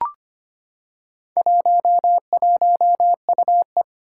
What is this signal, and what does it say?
Morse code, heard as a keyed tone, about 25 wpm, 700 Hz. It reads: 11UE
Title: Four Challenging Letter-Number Combinations 25wpm